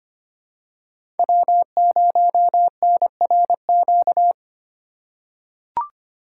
Morse code, heard as a keyed tone, about 25 wpm, 700 Hz. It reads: W0NRQ
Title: USA Call Signs - 5 Characters - 4x Character Spacing - 25wpm